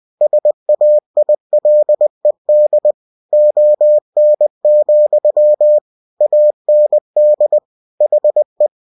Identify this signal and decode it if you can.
Morse code, heard as a keyed tone, about 20 wpm, 600 Hz. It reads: SAILED ON, AND HE
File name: cw-books / winnie-the-pooh / winnie-20wpm-page-0103